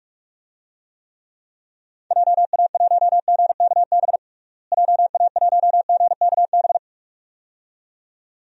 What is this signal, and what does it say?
Morse code, heard as a keyed tone, about 46 wpm, 700 Hz. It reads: JA1GKB JA1GKB
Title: DXCC-100 Call Signs - 6 Characters - Repeated 2x - 45wpm